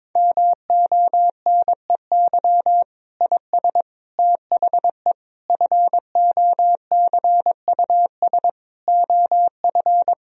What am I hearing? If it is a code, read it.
Morse code, heard as a keyed tone, about 22 wpm, 700 Hz. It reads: MONEY IS THE FOCUS OF
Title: Sentences from Top 1000 Words - Review Part 4 - 22wpm